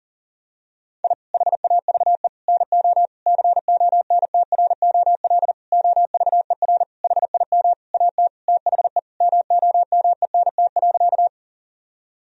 Morse code, heard as a keyed tone, about 40 wpm, 700 Hz. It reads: I HAVE NO CONTROL OVER HIM AT THE MOMENT.
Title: Sentences from Top 1000 Words - Review Part 3 - 40wpm